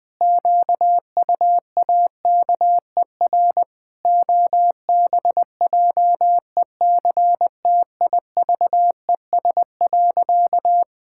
Morse code, heard as a keyed tone, about 20 wpm, 700 Hz. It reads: QUAKER OBJECTIVES.
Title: Pangram Challenge v2 - 20wpm